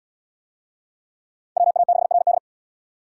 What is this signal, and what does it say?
Morse code, heard as a keyed tone, about 68 wpm, 690 Hz. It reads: WA6RD